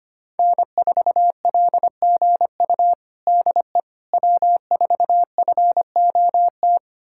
Morse code, heard as a keyed tone, about 25 wpm, 700 Hz. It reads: N4LGU DE W4FOT
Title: USA Callsign DE Callsign - 5 Characters - 25wpm